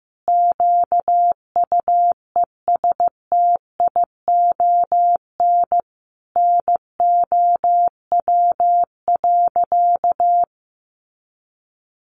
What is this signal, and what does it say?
Morse code, heard as a keyed tone, about 15 wpm, 700 Hz. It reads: QUESTION NOW.
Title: Sentences from Top 600 Words 15wpm